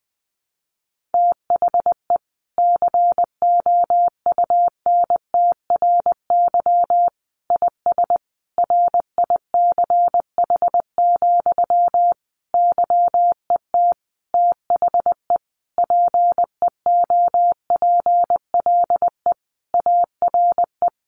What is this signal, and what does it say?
Morse code, heard as a keyed tone, about 20 wpm, 700 Hz. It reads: THE COUNTRY IS RICH, YET THE PEOPLE ARE
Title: Sentences from Top 1100 Words 20wpm